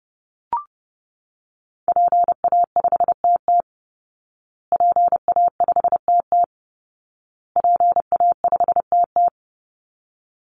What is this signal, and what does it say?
Morse code, heard as a keyed tone, about 30 wpm, 700 Hz. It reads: PA5TT PA5TT PA5TT
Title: Call Signs Repeated 2x - 30wpm